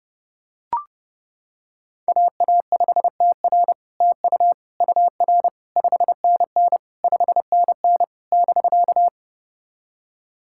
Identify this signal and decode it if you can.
Morse code, heard as a keyed tone, about 30 wpm, 700 Hz. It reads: AA5TR TU UR 5NN 5NN <BK>
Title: SOTA QSO Practice 30wpm